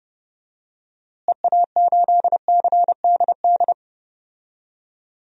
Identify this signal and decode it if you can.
Morse code, heard as a keyed tone, about 32 wpm, 700 Hz. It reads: EA8CDD
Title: DXCC-100 Call Signs - 6 Characters - Repeated 3x - 30wpm